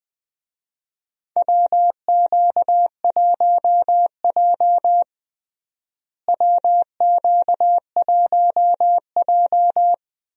Morse code, heard as a keyed tone, about 21 wpm, 700 Hz. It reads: WQ1J WQ1J
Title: USA Call Signs - 4 Characters - Repeated 3x - 20wpm